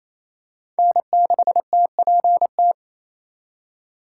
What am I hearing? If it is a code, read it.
Morse code, heard as a keyed tone, about 28 wpm, 700 Hz. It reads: N6TPT
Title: USA Call Signs - 5 Characters - 4x Character Spacing - 28wpm